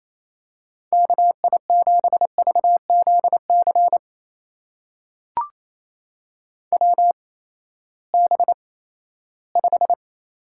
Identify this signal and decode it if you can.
Morse code, heard as a keyed tone, about 28 wpm, spaced as for 16 wpm, 700 Hz. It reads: KI7VZC WB5
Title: USA Call Signs - 6 Characters - 8x Character Spacing - 28wpm